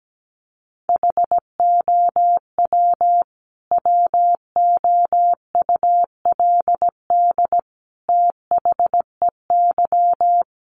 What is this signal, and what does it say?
Morse code, heard as a keyed tone, about 17 wpm, 700 Hz. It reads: HOW WOULD THEY
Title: Sentences from Top 300 Words 17wpm